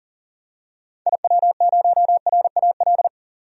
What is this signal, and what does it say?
Morse code, heard as a keyed tone, about 40 wpm, 700 Hz. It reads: IW0RAL
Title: DXCC-100 Call Signs - 6 Characters - 40wpm